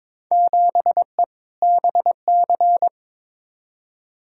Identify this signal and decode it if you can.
Morse code, heard as a keyed tone, about 22 wpm, 700 Hz. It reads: 7E BC
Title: Field Day Exchanges Encore - 22wpm